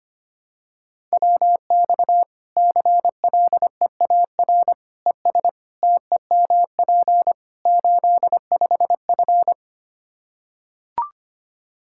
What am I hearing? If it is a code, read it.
Morse code, heard as a keyed tone, about 25 wpm, 700 Hz. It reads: WX CLEAR ES TEMP 85F
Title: Practice QSO - Exchange 2 - Info - 25wpm